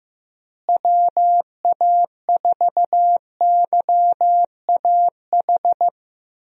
Morse code, heard as a keyed tone, about 16 wpm, 700 Hz. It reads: WA4YAH
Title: USA Call Signs - 6 Characters - Repeated 2x - 15wpm